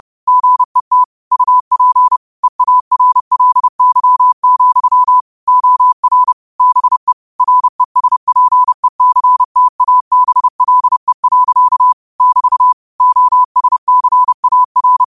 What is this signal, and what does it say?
Morse code, heard as a keyed tone, about 30 wpm, 1000 Hz. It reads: GET UP EARLY, OR BE RESPECTABLE. <BT> OSCAR
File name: qotdfiles / 30wpm